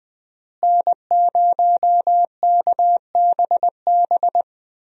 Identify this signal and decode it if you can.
Morse code, heard as a keyed tone, about 20 wpm, 700 Hz. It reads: N0KBB
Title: USA Call Signs - 5 Characters - Repeated 2x - 20wpm